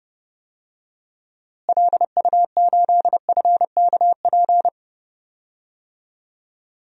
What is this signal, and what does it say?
Morse code, heard as a keyed tone, about 32 wpm, 700 Hz. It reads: LU8FKP
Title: DXCC-100 Call Signs - 6 Characters - Repeated 3x - 30wpm